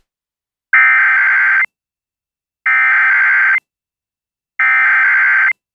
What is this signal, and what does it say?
Emergency Alert System (SAME) tones (header bursts), heard as harsh buzzing bursts. It is heard from 0.7 to 5.5 s.